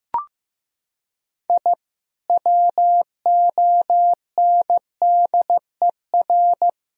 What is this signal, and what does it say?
Morse code, heard as a keyed tone, about 15 wpm, 700 Hz. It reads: I WONDER
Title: Sentences from Top 1000 Words - Review Part 3 - 15wpm